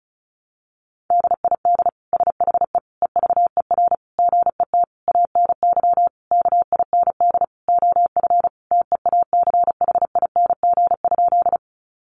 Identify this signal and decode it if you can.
Morse code, heard as a keyed tone, about 34 wpm, 700 Hz. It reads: DID SHE EVER GET ANY KIND OF TEACHING?
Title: Sentences from Top 1100 Words 35wpm